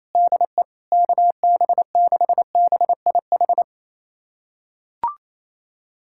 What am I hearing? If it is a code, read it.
Morse code, heard as a keyed tone, about 28 wpm, 700 Hz. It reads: DE KB6BIH
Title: USA Callsign DE Callsign - 6 Characters - 28wpm